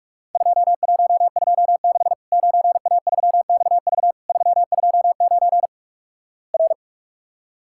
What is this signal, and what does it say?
Morse code, heard as a keyed tone, about 46 wpm, 700 Hz. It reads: 2126 9A3XV 329 R
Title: Worked All Europe DX Contest - QTC Exchange - 45wpm